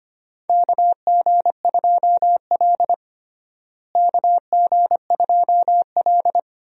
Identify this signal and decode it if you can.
Morse code, heard as a keyed tone, about 25 wpm, 700 Hz. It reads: KG2L KG2L
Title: USA Call Signs - 4 Characters - Repeated 2x - 25wpm